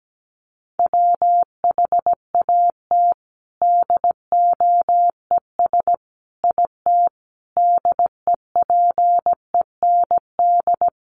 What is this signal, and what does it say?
Morse code, heard as a keyed tone, about 17 wpm, 700 Hz. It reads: WHAT DOES IT DEPEND
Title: Sentences from Top 1300 Words 17wpm